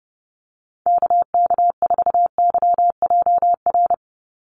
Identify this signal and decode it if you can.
Morse code, heard as a keyed tone, about 30 wpm, 700 Hz. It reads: KK4YJR
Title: USA Call Signs 30wpm